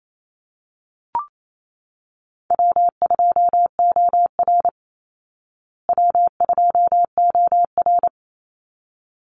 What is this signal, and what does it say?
Morse code, heard as a keyed tone, about 28 wpm, 700 Hz. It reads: W2OR W2OR
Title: Easy Call Signs Repeated 2x - 28wpm